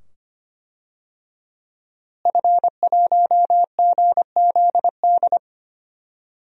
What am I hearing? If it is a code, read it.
Morse code, heard as a keyed tone, about 25 wpm, 700 Hz. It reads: F1GZD
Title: DXCC-100 Call Signs - 5 Characters - Repeated 2x - 25wpm